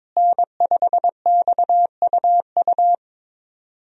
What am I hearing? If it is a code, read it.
Morse code, heard as a keyed tone, about 22 wpm, 700 Hz. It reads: N5XUU
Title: USA Call Signs - 5 Characters - Repeated 2x - 22wpm